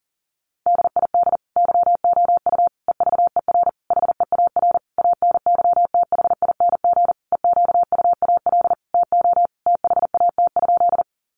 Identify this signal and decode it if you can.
Morse code, heard as a keyed tone, about 40 wpm, 700 Hz. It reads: DID YOU EVER HEAR ANYTHING EQUAL TO THAT?
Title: Sentences from Top 1400 Words 40wpm